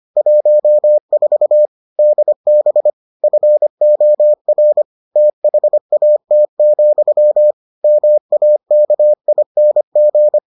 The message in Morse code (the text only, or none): 14 DB FOR THAT, MAKING